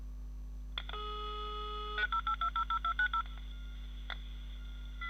DTMF, heard as telephone touch tones, tones of about 90 ms. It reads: 08000
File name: audio-modem-conexao-discada